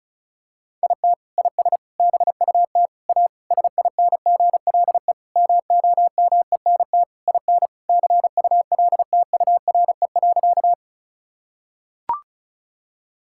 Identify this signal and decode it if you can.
Morse code, heard as a keyed tone, about 34 wpm, 700 Hz. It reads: IT IS BUT A SINGLE MOMENT IN CULTURE.
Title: Sentences from Top 1000 Words - Review Part 6 - 35wpm